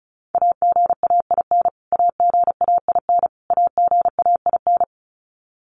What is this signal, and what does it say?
Morse code, heard as a keyed tone, about 34 wpm, 700 Hz. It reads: AGAIN AGAIN AGAIN
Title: Top 300 Words Repeated 35wpm